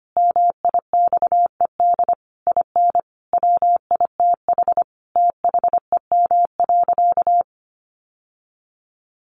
Morse code, heard as a keyed tone, about 25 wpm, 700 Hz. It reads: MIXED IN WITH THEM.
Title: Sentences from Top 1500 Words 25wpm